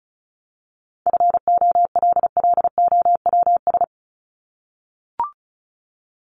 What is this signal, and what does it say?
Morse code, heard as a keyed tone, about 34 wpm, 700 Hz. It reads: FOLLOWS
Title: ES:S Suffix - Top 100 Words with Base Words 35wpm